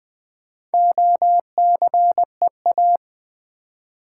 Morse code, heard as a keyed tone, about 20 wpm, 700 Hz. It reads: OCEA
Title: Build A Word - 1 Letter At A Time - 5-Letter Words - 20wpm